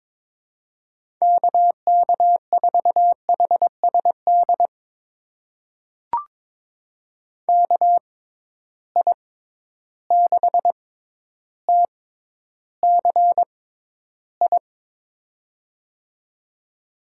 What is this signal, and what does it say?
Morse code, heard as a keyed tone, about 22 wpm, spaced as for 11 wpm, 700 Hz. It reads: KK4HSD KI6TCI
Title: USA Call Signs - 6 Characters - 6x Character Spacing - 22wpm